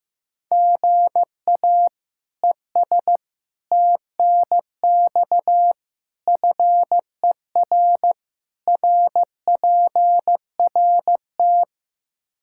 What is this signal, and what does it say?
Morse code, heard as a keyed tone, about 15 wpm, 700 Hz. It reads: GA ES TNX FER RPRT
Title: Practice QSO - Exchange 1 - Beginning - 15wpm